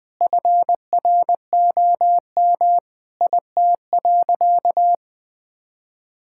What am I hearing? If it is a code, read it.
Morse code, heard as a keyed tone, about 20 wpm, 700 Hz. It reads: FROM IT.
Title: Sentences from Top 1000 Words - Review Part 5 - 20wpm